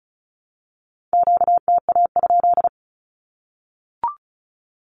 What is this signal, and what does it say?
Morse code, heard as a keyed tone, about 34 wpm, 700 Hz. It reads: QTA?
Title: Q-Codes Repeated 3x 35wpm